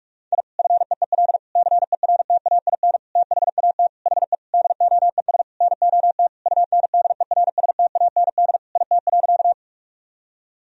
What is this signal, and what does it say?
Morse code, heard as a keyed tone, about 46 wpm, 700 Hz. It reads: I FEEL CERTAIN THAT HE DOES NOT UNDERSTAND IT.
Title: Sentences from Top 1000 Words - Review Part 2 - 45wpm